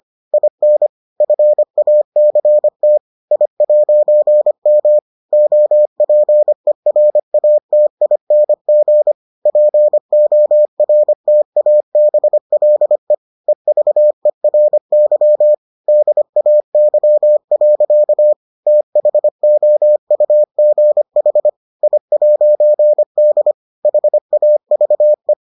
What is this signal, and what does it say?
Morse code, heard as a keyed tone, about 25 wpm, 600 Hz. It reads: IN FACT I'M OPERATING PORTABLE EVERY DAY. THOUGH I'D HAVE